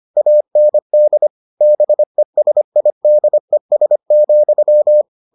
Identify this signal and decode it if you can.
Morse code, heard as a keyed tone, about 25 wpm, 600 Hz. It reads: AND BESIDES,